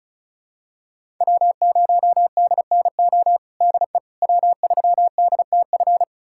Morse code, heard as a keyed tone, about 36 wpm, 700 Hz. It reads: W0DNO DE W3DTF
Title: USA Callsign DE Callsign - 5 Characters - 35wpm